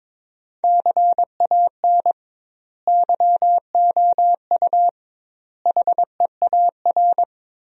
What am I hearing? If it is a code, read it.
Morse code, heard as a keyed tone, about 22 wpm, 700 Hz. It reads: CAN YOU HEAR
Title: 2x Word-Spacing - Sentences from Top 300 Words - 22wpm